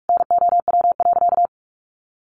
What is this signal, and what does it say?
Morse code, heard as a keyed tone, about 44 wpm, 700 Hz. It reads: NOW.
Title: Sentences from Top 1300 Words 45wpm